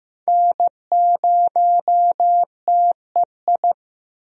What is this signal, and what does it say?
Morse code, heard as a keyed tone, about 15 wpm, 700 Hz. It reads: N0TEI
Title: USA Call Signs - 5 Characters - Repeated 3x - 15wpm